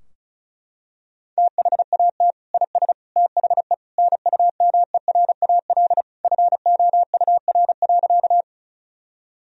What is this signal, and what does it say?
Morse code, heard as a keyed tone, about 36 wpm, 700 Hz. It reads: THAT IS THE NUMERAL FOUR.
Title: Sentences from Top 1000 Words - Review Part 2 - 35wpm